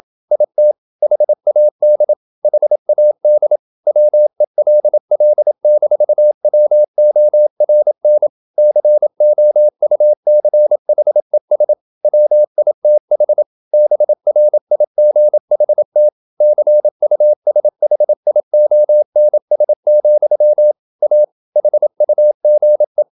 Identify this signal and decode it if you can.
Morse code, heard as a keyed tone, about 27 wpm, 600 Hz. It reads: IT HAD HAD WELL-WORN COUCHES WITH BRIGHT CUSHIONS, A HUGE
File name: Never 0027wpm